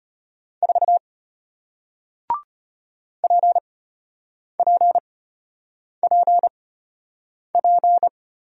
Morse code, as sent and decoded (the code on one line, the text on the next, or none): ....- / .--. .--. .--. .--.
4 PPPP